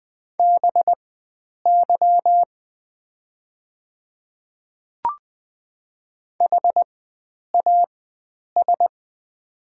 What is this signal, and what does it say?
Morse code, heard as a keyed tone, about 20 wpm, spaced as for 9 wpm, 690 Hz. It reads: BY HAS
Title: 4x Character Spacing - Top 100 Words - 20wpm